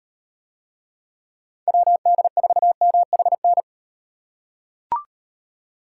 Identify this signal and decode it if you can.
Morse code, heard as a keyed tone, about 38 wpm, 700 Hz. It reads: WD4MHN